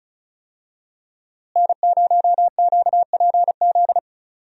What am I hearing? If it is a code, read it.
Morse code, heard as a keyed tone, about 34 wpm, 700 Hz. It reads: N0QPZ
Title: USA Call Signs - 5 Characters - Repeated 3x - 35wpm